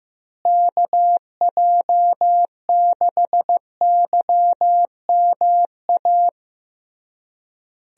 Morse code, heard as a keyed tone, about 15 wpm, 700 Hz. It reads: KJ6YMA
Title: USA Call Signs - 6 Characters - Repeated 3x - 15wpm